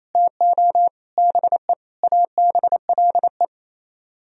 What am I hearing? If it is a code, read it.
Morse code, heard as a keyed tone, about 35 wpm, 700 Hz. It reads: TO BE ABLE